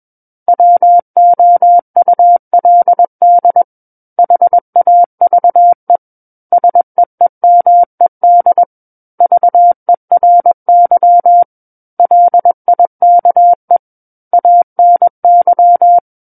Morse code, heard as a keyed tone, about 21 wpm, 700 Hz. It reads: WOULD HAVE SEEMED VERY LIKE ANY
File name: Book One / 12-Chapter 12 - 21 WPM